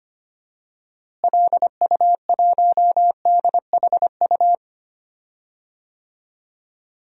Morse code, heard as a keyed tone, about 25 wpm, 700 Hz. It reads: LU1DHU
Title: DXCC-100 Call Signs - 6 Characters - Repeated 2x - 25wpm